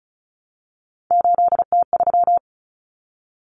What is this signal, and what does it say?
Morse code, heard as a keyed tone, about 34 wpm, 700 Hz. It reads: 8T3
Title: Common Cut Numbers - 3-Digit 35wpm